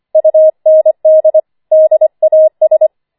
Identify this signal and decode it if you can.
Morse code, heard as a keyed tone, about 25 wpm, 630 Hz. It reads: UND DAS
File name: 25wpm